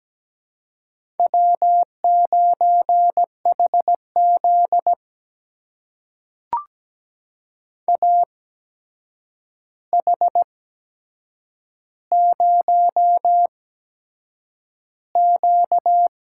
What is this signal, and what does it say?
Morse code, heard as a keyed tone, about 17 wpm, spaced as for 7 wpm, 700 Hz. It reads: W9HZ AH0Q
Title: USA Call Signs - 4 Characters - 8x Character Spacing - 17wpm